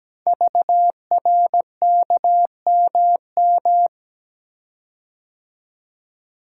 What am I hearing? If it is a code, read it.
Morse code, heard as a keyed tone, about 17 wpm, 700 Hz. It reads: VRKMM
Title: Code Groups - 5 Letters - 17wpm